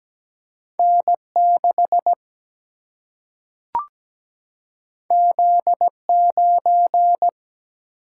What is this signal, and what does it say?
Morse code, heard as a keyed tone, about 17 wpm, 700 Hz. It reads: N 6 Z 9
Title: Single-Letter Single-Number 17wpm